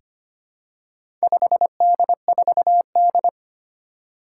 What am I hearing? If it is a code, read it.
Morse code, heard as a keyed tone, about 25 wpm, 700 Hz. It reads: 5D4D
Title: Four Challenging Letter-Number Combinations 25wpm